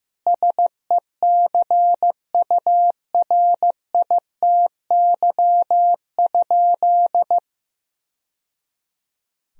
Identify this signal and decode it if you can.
Morse code, heard as a keyed tone, about 16 wpm, 700 Hz. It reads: SECURITY?
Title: Sentences from Top 1000 Words - Review Part 4 - 15wpm